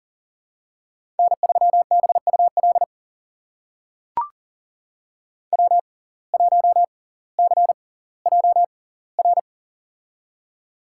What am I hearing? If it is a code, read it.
Morse code, heard as a keyed tone, about 40 wpm, spaced as for 20 wpm, 700 Hz. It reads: N3BUL W1CJR
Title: USA Call Signs - 5 Characters - 6x Character Spacing - 40wpm